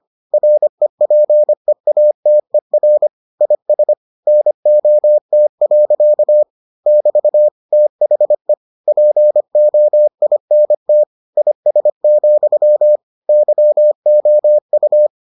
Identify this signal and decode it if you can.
Morse code, heard as a keyed tone, about 25 wpm, 600 Hz. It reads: REPEATER IS NOT. <BT> THE POINT IS, YOU